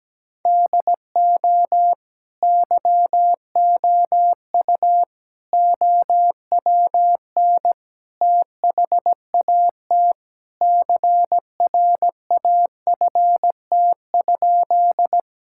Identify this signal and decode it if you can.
Morse code, heard as a keyed tone, about 17 wpm, 700 Hz. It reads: DO YOU OWN THAT CRAFT?
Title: Sentences from Top 1000 Words - Review Part 6 - 17wpm